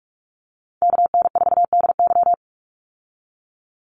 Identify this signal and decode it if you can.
Morse code, heard as a keyed tone, about 46 wpm, 700 Hz. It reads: KN4DY
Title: USA Call Signs 45wpm